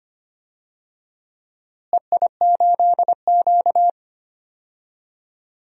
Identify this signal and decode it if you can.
Morse code, heard as a keyed tone, about 25 wpm, 700 Hz. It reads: EI8Q
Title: DXCC-100 Call Signs - 4 Characters - Repeated 2x - 25wpm